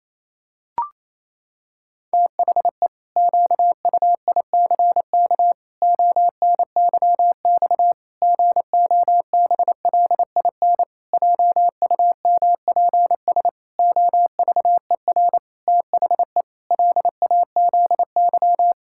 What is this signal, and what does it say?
Morse code, heard as a keyed tone, about 27 wpm, 700 Hz. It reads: THE QUICK ONYX GOBLIN JUMPS OVER THE LAZY
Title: Pangram Challenge v2 - 28wpm